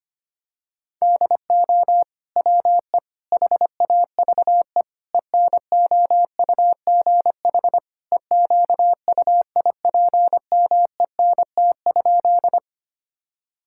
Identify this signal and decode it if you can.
Morse code, heard as a keyed tone, about 25 wpm, 700 Hz. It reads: DO WE HAVE ENOUGH EQUIPMENT?
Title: Sentences from Top 1000 Words - Review Part 5 - 25wpm